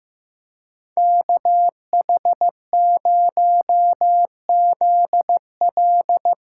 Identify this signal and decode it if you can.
Morse code, heard as a keyed tone, about 16 wpm, 700 Hz. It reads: KH0ZL
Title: USA Call Signs - 5 Characters - 3x Character Spacing - 15wpm